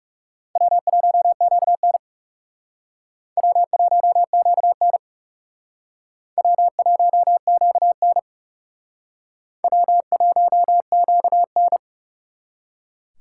Morse code, heard as a keyed tone, about 36 wpm, 700 Hz. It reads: W1QN W1QN W1QN W1QN
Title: Speed Racing - USA 1x2 Call Signs 30wpm